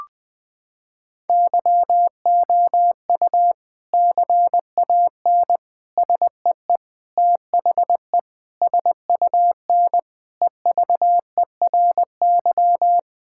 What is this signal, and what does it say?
Morse code, heard as a keyed tone, about 20 wpm, 700 Hz. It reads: YOU CAN SEE THE SUN EVERY
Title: Sentences from Top 1600 Words 20wpm